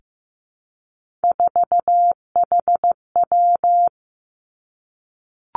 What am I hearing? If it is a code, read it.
Morse code, heard as a keyed tone, about 15 wpm, 700 Hz. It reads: 4HW
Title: Three Challenging Letter-Number Combinations 15wpm